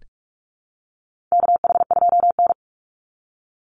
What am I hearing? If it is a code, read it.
Morse code, heard as a keyed tone, about 44 wpm, 700 Hz. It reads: KH2N
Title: Call Signs Repeated 3x - 45wpm